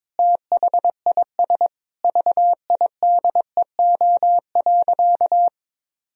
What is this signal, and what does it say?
Morse code, heard as a keyed tone, about 22 wpm, 700 Hz. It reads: THIS VIDEO.
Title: Sentences from Top 1000 Words - Review Part 4 - 22wpm